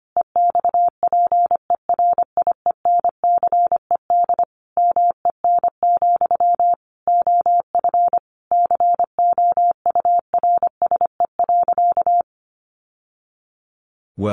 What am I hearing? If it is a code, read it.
Morse code, heard as a keyed tone, about 25 wpm, 700 Hz. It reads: EXPERIENCED MEN, OF COURSE.
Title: Sentences from Top 1100 Words 25wpm